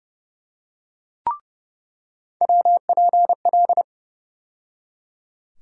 Morse code, heard as a keyed tone, about 30 wpm, 700 Hz. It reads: WPL
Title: USA Call Sign Postfixes - 3 Letters 30wpm